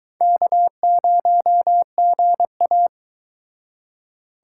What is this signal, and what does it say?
Morse code, heard as a keyed tone, about 23 wpm, 700 Hz. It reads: K0GA